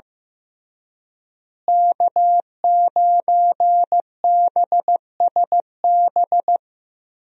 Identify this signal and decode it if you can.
Morse code, heard as a keyed tone, about 15 wpm, 700 Hz. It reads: K9BSB
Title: USA Call Signs - 5 Characters - Repeated 3x - 15wpm